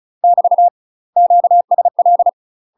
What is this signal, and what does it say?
Morse code, heard as a keyed tone, about 34 wpm, 700 Hz. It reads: <BT> QSL
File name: UK MASTER QSOs TRAINING / 35WPM / UK MASTER QSOs 0003